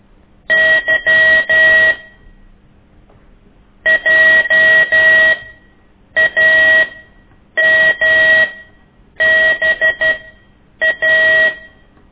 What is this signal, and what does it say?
Morse code, heard as a keyed tone, about 12 wpm, spaced as for 8 wpm, 1900 Hz. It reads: Y JAMBA
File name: ik1weg2 / CORSI CW